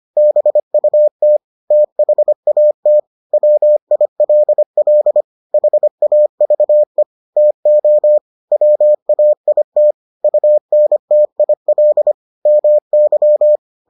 Morse code, heard as a keyed tone, about 25 wpm, 600 Hz. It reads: BUT THAT WILL HAVE TO WAIT UNTIL MY